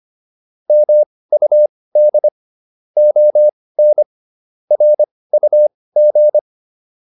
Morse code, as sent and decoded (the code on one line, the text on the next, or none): -- ..- -.. / --- -. / .-. ..- --.
MUD ON RUG